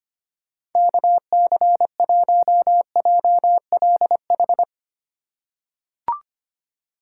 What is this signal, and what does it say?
Morse code, heard as a keyed tone, about 25 wpm, 700 Hz. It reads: KC1JLH
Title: USA Call Signs - 6 Characters - 2x Character Spacing - 25wpm